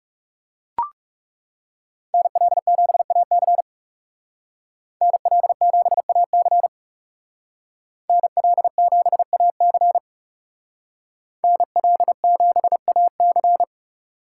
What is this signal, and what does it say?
Morse code, heard as a keyed tone, about 38 wpm, 700 Hz. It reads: NL7AC NL7AC NL7AC NL7AC